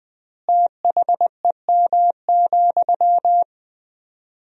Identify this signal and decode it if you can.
Morse code, heard as a keyed tone, about 21 wpm, 700 Hz. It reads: THEM,
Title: 4x Word-Spacing - Sentences from Top 400 Words - 20wpm